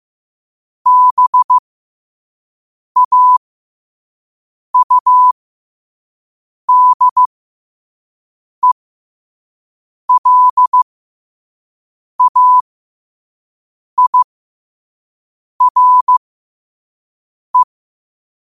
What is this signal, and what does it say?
Morse code, heard as a keyed tone, about 15 wpm, spaced as for 5 wpm, 1000 Hz. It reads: BAUDELAIRE
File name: qotdfiles / 5wpm